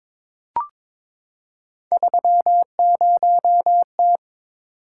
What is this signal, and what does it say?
Morse code, heard as a keyed tone, about 22 wpm, 700 Hz. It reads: 30T
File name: practice / 22wpm / Three Letter-Number Combinations - 22wpm